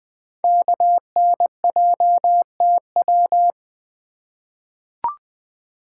Morse code, heard as a keyed tone, about 20 wpm, 700 Hz. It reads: KNJTW
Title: Code Groups - 5 Letter:Numbers - 20wpm